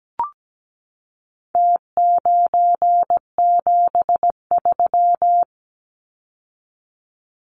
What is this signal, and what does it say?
Morse code, heard as a keyed tone, about 17 wpm, 700 Hz. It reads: T973
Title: Numbers - Contest Style - 4-Digit 17wpm